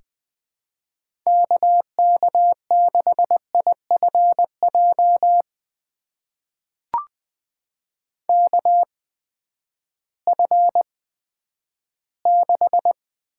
Morse code, heard as a keyed tone, about 20 wpm, spaced as for 11 wpm, 700 Hz. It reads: KK6IFJ KF6
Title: USA Call Signs - 6 Characters - 8x Character Spacing - 20wpm